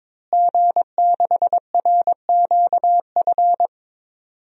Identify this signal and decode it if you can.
Morse code, heard as a keyed tone, about 22 wpm, 700 Hz. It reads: G6RQF
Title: DXCC-100 Call Signs - 5 Characters - Repeated 3x - 22wpm